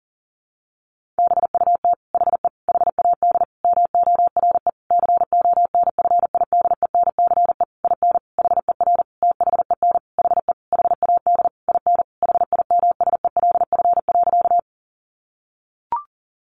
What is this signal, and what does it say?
Morse code, heard as a keyed tone, about 40 wpm, 700 Hz. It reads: BUT HE HAD MORE CONFIDENCE IN HER THEN HE HAD IN HIMSELF.
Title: Sentences from Top 1500 Words 40wpm